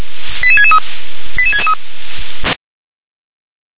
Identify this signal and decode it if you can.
Five-tone selective call (ZVEI), heard as quick runs of five tones, tones of about 70 ms. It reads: 88663 88663